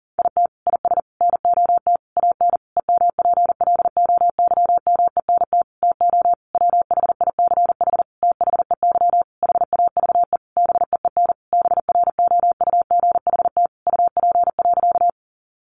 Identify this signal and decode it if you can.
Morse code, heard as a keyed tone, about 40 wpm, 700 Hz. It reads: IT IS NOT AN EMPLOYMENT TO WHICH THEY HAVE BEEN BROUGHT UP.
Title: Sentences from Top 1500 Words 40wpm